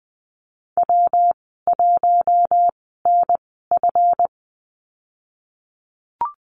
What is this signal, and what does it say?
Morse code, heard as a keyed tone, about 20 wpm, spaced as for 14 wpm, 700 Hz. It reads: W1NF
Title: 245 Lesson 39g - Letter z - Call Signs 20wpm